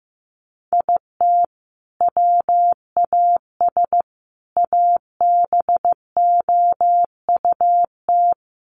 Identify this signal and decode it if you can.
Morse code, heard as a keyed tone, about 15 wpm, 700 Hz. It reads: IT WAS ABOUT
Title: Sentences from Top 1500 Words 15wpm